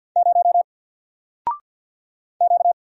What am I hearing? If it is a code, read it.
Morse code, heard as a keyed tone, about 50 wpm, 700 Hz. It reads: 0 Q
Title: Single Character (Letter:Number:Punctuation:Prosign) 50wpm